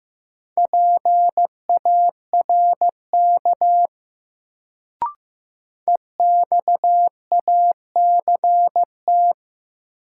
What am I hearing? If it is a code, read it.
Morse code, heard as a keyed tone, about 15 wpm, 700 Hz. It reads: PARK EXACT
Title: Sentences from Top 1300 Words - New Word Drill - 15wpm